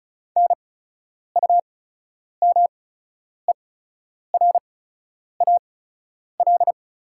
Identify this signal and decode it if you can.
Morse code, heard as a keyed tone, about 36 wpm, spaced as for 10 wpm, 700 Hz. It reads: NUMERAL
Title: 8x Character Spacing - Top 500 Words - 35wpm